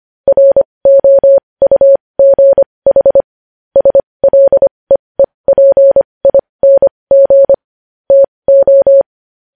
Morse code, heard as a keyed tone, about 25 wpm, spaced as for 20 wpm, 550 Hz. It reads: ROUGH SLEEPING TO
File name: podcast / News headlines-1x-25x20